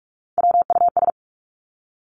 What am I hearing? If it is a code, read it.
Morse code, heard as a keyed tone, about 46 wpm, 700 Hz. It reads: WUS
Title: Three Challenging Letter-Number Combinations 45wpm